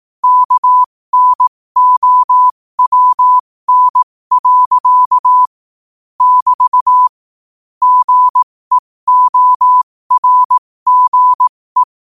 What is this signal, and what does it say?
Morse code, heard as a keyed tone, about 18 wpm, spaced as for 15 wpm, 1000 Hz. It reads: KNOWN. <BT> GEORGE
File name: qotdfiles / 15wpm